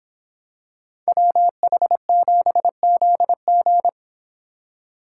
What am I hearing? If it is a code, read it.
Morse code, heard as a keyed tone, about 26 wpm, 700 Hz. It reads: WH7ZG